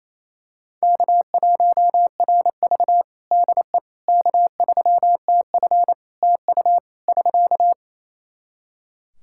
Morse code, heard as a keyed tone, about 28 wpm, 700 Hz. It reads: K1RV DE K3TF TU <SK>
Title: Practice QSO - Exchange 4 - Ending - 28wpm